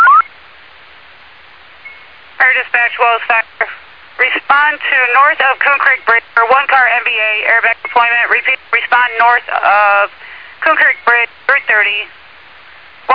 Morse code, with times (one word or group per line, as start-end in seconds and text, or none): none